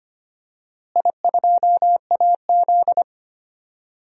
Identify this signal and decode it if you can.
Morse code, heard as a keyed tone, about 25 wpm, 700 Hz. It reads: I2AZ
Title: DXCC-100 Call Signs - 4 Characters - Repeated 2x - 25wpm